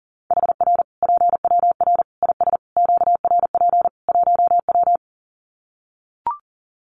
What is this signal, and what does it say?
Morse code, heard as a keyed tone, about 40 wpm, 700 Hz. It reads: HR PWR IS QRP 1W
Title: Common QSO Phrases 40wpm